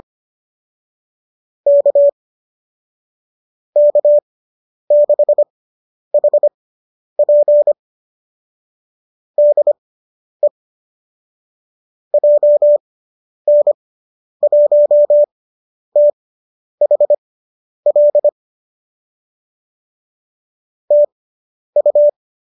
Morse code, heard as a keyed tone, about 25 wpm, spaced as for 10 wpm, 590 Hz. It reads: K K6HP DE JN1THL TU